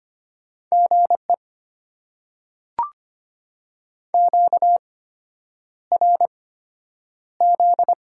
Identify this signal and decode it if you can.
Morse code, heard as a keyed tone, about 25 wpm, spaced as for 9 wpm, 700 Hz. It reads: GE QRZ
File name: Top 90 QSO Elements - 8x Character Spacing - 25wpm